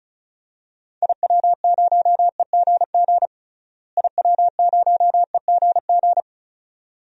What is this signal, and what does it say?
Morse code, heard as a keyed tone, about 34 wpm, 700 Hz. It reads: IW0EGG IW0EGG
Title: DXCC-100 Call Signs - 6 Characters - Repeated 2x - 35wpm